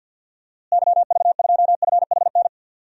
Morse code, heard as a keyed tone, about 52 wpm, 700 Hz. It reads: YV2FHN
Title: DXCC-100 Call Signs - 6 Characters - Repeated 3x - 50wpm